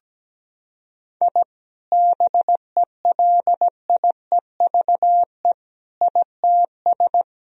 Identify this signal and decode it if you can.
Morse code, heard as a keyed tone, about 17 wpm, 700 Hz. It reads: I BELIEVE ITS
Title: Sentences from Top 1000 Words - Review Part 3 - 17wpm